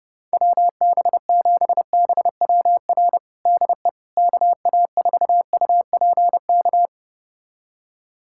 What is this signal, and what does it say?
Morse code, heard as a keyed tone, about 30 wpm, 700 Hz. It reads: WB7BWR DE KA4UPK
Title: USA Callsign DE Callsign 30wpm